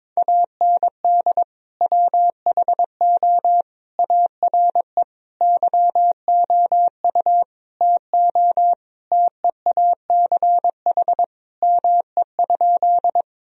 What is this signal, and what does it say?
Morse code, heard as a keyed tone, about 22 wpm, 700 Hz. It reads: AND WHO ARE YOU TO TEACH ME?
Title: Sentences from Top 1000 Words - Review Part 2 - 22wpm